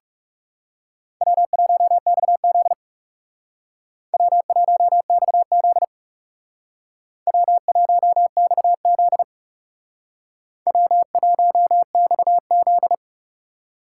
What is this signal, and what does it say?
Morse code, heard as a keyed tone, about 38 wpm, 700 Hz. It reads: W1XZ W1XZ W1XZ W1XZ